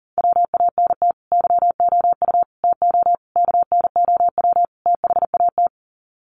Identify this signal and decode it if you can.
Morse code, heard as a keyed tone, about 40 wpm, 700 Hz. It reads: WANT YOU TO KNOW THAT
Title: Sets of 5 Words Encore 40wpm